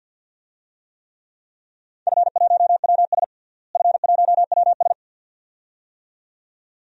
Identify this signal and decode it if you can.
Morse code, heard as a keyed tone, about 51 wpm, 700 Hz. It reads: F1WS F1WS
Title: DXCC-100 Call Signs - 4 Characters - Repeated 2x - 50wpm